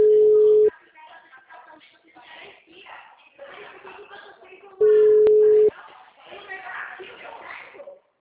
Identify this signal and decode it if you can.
Ringback tone (Europe, 425 Hz), with speech.